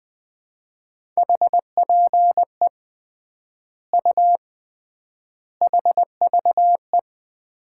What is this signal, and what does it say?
Morse code, heard as a keyed tone, about 20 wpm, 700 Hz. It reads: HPE U HVE
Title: Common QSO Phrases - 3x Word-Spacing - 20wpm